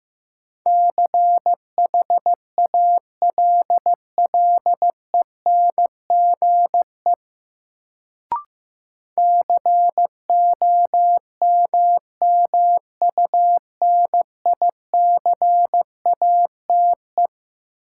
Morse code, heard as a keyed tone, about 15 wpm, 700 Hz. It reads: CHALLENGE COMMUNICATE
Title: Sentences from Top 1300 Words - New Word Drill Alt - 15wpm